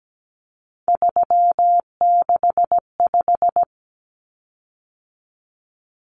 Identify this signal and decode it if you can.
Morse code, heard as a keyed tone, about 17 wpm, 700 Hz. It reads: 365
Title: Common Cut Numbers - 3-Digit 17wpm